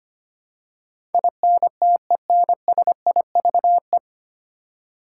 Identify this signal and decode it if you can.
Morse code, heard as a keyed tone, about 25 wpm, 700 Hz. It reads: INTENSIVE
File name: Nine Letter Words - 25wpm